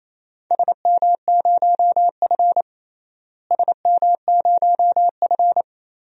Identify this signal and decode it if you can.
Morse code, heard as a keyed tone, about 28 wpm, 700 Hz. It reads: SM0F SM0F
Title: DXCC-100 Call Signs - 4 Characters - Repeated 2x - 28wpm